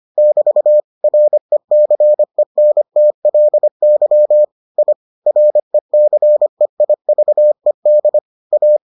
Morse code, heard as a keyed tone, about 25 wpm, 600 Hz. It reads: <BT> RECENTLY I RECEIVED A